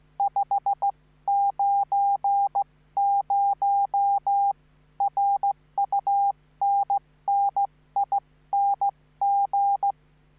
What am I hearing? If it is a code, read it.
Morse code, heard as a keyed tone, about 15 wpm, spaced as for 13 wpm, 800 Hz. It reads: 590 RUNNING